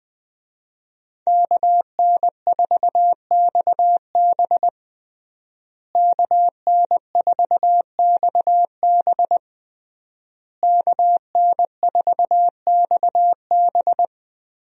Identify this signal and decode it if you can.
Morse code, heard as a keyed tone, about 20 wpm, 700 Hz. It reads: KN4XB KN4XB KN4XB
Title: USA Call Signs - 5 Characters - Repeated 3x - 20wpm